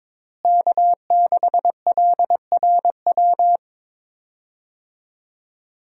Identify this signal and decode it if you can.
Morse code, heard as a keyed tone, about 22 wpm, 700 Hz. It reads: K6LRW
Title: USA Call Signs - 5 Characters - Repeated 2x - 22wpm